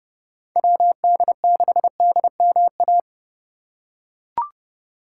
Morse code, heard as a keyed tone, about 30 wpm, 700 Hz. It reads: WD6DMA